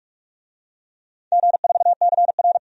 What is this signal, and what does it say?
Morse code, heard as a keyed tone, about 46 wpm, 700 Hz. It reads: G4CR
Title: DXCC-100 Call Signs - 4 Characters - 45wpm